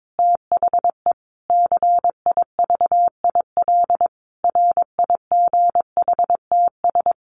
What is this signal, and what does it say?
Morse code, heard as a keyed tone, about 22 wpm, 700 Hz. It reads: THE CIVIL RIGHTS
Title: Sets of 5 Words Encore2 22wpm